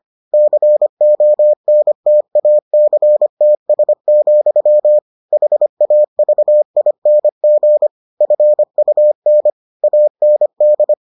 Morse code, heard as a keyed tone, about 25 wpm, 600 Hz. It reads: CONTACTS, HAVING FUN AND